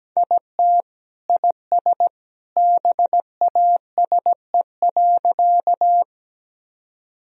Morse code, heard as a keyed tone, about 17 wpm, 700 Hz. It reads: IT IS BASE.
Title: Sentences from Top 1000 Words - Review Part 1 - 17wpm